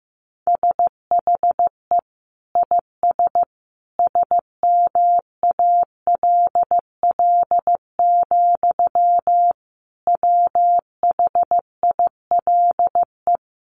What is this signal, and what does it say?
Morse code, heard as a keyed tone, about 14 wpm, 700 Hz. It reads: SHE IS SMALL, WHILE
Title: Sentences from Top 1300 Words 15wpm